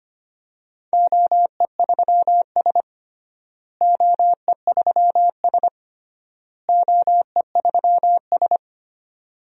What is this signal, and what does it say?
Morse code, heard as a keyed tone, about 25 wpm, 700 Hz. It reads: OE3S OE3S OE3S
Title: DXCC-100 Call Signs - 4 Characters - Repeated 3x - 25wpm